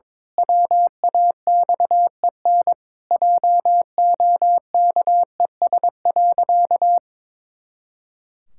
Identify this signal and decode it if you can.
Morse code, heard as a keyed tone, about 22 wpm, 700 Hz. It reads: WAXEN JOKES.
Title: Pangram Challenge v2 - 22wpm